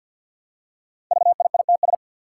Morse code, heard as a keyed tone, about 51 wpm, 700 Hz. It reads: VIITS
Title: Five Challenging Letter-Number Combinations 50wpm